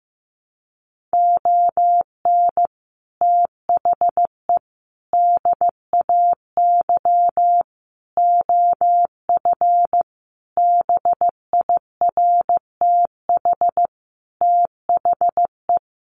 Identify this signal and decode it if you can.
Morse code, heard as a keyed tone, about 15 wpm, 700 Hz. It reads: ON THE DAY OF BIRTH THE
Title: Sentences from Top 1100 Words 15wpm